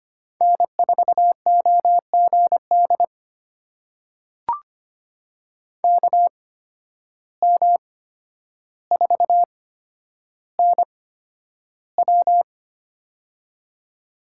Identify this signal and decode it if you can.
Morse code, heard as a keyed tone, about 25 wpm, spaced as for 11 wpm, 700 Hz. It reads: N4OGD KM4NW
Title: USA Call Signs - 5 Characters - 8x Character Spacing - 25wpm